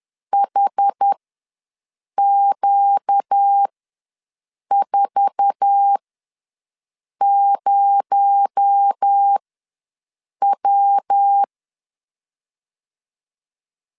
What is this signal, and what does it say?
Morse code, heard as a keyed tone, about 10 wpm, spaced as for 6 wpm, 800 Hz. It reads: HQ40W
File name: LEZIONI / 8 YJ45 / 8 YJ4 ws 10wpm 4020